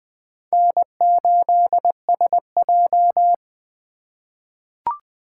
Morse code, heard as a keyed tone, about 21 wpm, 700 Hz. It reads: N8SJ
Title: USA Call Signs - 4 Characters - 2x Character Spacing - 20wpm